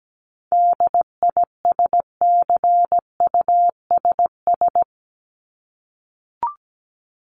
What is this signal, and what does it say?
Morse code, heard as a keyed tone, about 17 wpm, 700 Hz. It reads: DISCUSS
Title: Top 751-1000 Words 17wpm